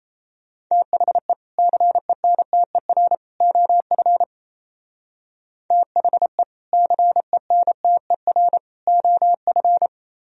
Morse code, heard as a keyed tone, about 35 wpm, 700 Hz. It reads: THE CENTER OF THE CENTER OF